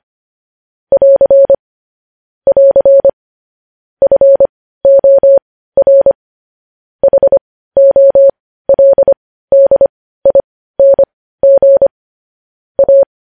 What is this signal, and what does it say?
Morse code, heard as a keyed tone, about 25 wpm, spaced as for 15 wpm, 550 Hz. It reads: <AR> <AR> FOR HOLDING A
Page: news headlines in morse code at 15 wpm